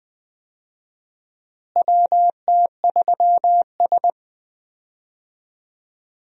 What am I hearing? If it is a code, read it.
Morse code, heard as a keyed tone, about 20 wpm, 700 Hz. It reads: WT3S
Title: USA Call Signs - 4 Characters - Repeated 3x - 20wpm